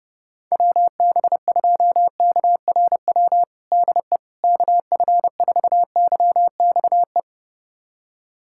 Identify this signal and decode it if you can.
Morse code, heard as a keyed tone, about 30 wpm, 700 Hz. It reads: WB2KRW DE KF4YXE
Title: USA Callsign DE Callsign - 6 Characters - 30wpm